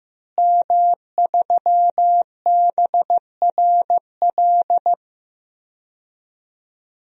Morse code, heard as a keyed tone, about 14 wpm, 700 Hz. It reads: M3BRL
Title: DXCC-100 Call Signs - 5 Characters - Repeated 3x - 15wpm